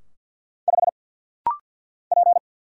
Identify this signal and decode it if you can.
Morse code, heard as a keyed tone, about 50 wpm, 700 Hz. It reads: 5 P
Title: Single Letter-Number - Rapid-Fire 50wpm